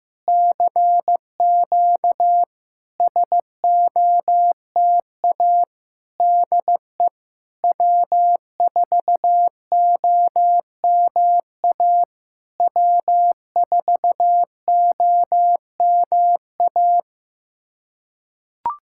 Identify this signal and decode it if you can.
Morse code, heard as a keyed tone, about 15 wpm, 700 Hz. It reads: CQ SOTA DE W4OMA W4OMA
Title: SOTA QSO Practice 15wpm